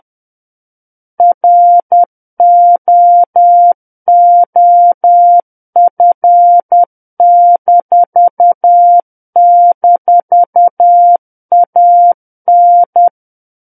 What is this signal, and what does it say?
Morse code, heard as a keyed tone, about 10 wpm, 700 Hz. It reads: ROOF--AN
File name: Book One / 01-Chapter 1 - 10 WPM